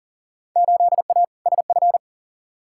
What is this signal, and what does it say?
Morse code, heard as a keyed tone, about 37 wpm, 700 Hz. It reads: 8A SF
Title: Field Day Exchanges Encore - 40wpm